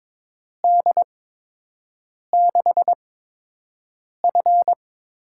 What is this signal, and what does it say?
Morse code, heard as a keyed tone, about 22 wpm, spaced as for 6 wpm, 700 Hz. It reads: D6F
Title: USA Call Signs - 8x Character Spacing - 22wpm